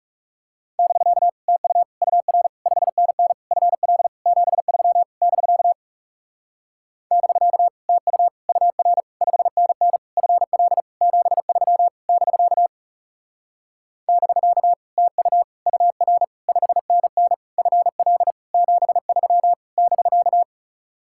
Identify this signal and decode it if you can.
Morse code, heard as a keyed tone, about 37 wpm, 700 Hz. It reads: <BK> TU UR 5NN FL 73 <BK> <BK> TU UR 5NN FL 73 <BK> <BK> TU UR 5NN FL 73 <BK>
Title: Speed-Racing Common QSO Phrases - 30wpm